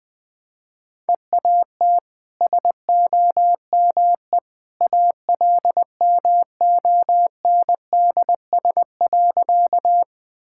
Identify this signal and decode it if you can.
Morse code, heard as a keyed tone, about 20 wpm, 700 Hz. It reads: EAT SOME ALMONDS.
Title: Easy Sentences with Hints - 20wpm